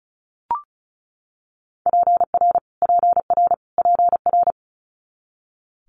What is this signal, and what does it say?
Morse code, heard as a keyed tone, about 36 wpm, 700 Hz. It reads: PR PR PR
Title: Multi-letter Phonemes Repeated 3x 35wpm